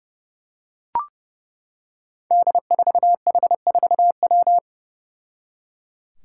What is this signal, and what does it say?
Morse code, heard as a keyed tone, about 31 wpm, 700 Hz. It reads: D4H4W
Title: Five Challenging Letter-Number Combinations 30wpm